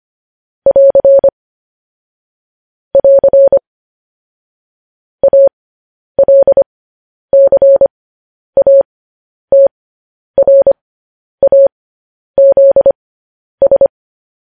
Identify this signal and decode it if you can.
Morse code, heard as a keyed tone, about 25 wpm, spaced as for 10 wpm, 550 Hz. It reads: <AR> <AR> ALCATRAZS
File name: podcast / News headlines-1x-25x10